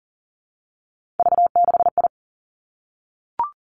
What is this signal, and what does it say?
Morse code, heard as a keyed tone, about 40 wpm, 700 Hz. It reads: V6I
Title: Three Challenging Letter-Number Combinations 40wpm